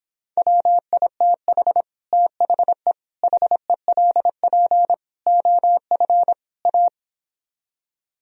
Morse code, heard as a keyed tone, about 31 wpm, 700 Hz. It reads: WITH THE HELP OF A